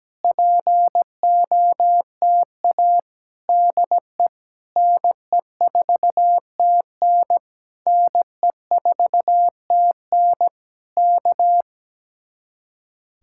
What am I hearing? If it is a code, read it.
Morse code, heard as a keyed tone, about 17 wpm, 700 Hz. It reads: POTA DE NE4TN NE4TN K
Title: Speed-Racing Common QSO Phrases Encore - 17wpm